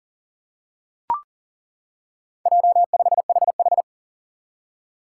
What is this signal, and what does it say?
Morse code, heard as a keyed tone, about 40 wpm, 700 Hz. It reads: J5HH
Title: Four Challenging Letter-Number Combinations 40wpm